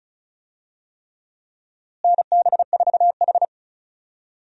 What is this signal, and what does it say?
Morse code, heard as a keyed tone, about 34 wpm, 700 Hz. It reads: NB4H
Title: USA Call Signs - 4 Characters - Repeated 3x - 35wpm